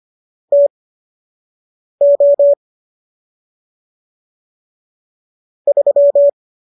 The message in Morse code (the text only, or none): TO 3